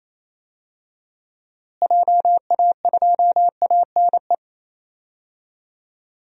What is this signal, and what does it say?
Morse code, heard as a keyed tone, about 28 wpm, 700 Hz. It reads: JA2ANE
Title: DXCC-100 Call Signs - 6 Characters - Repeated 3x - 28wpm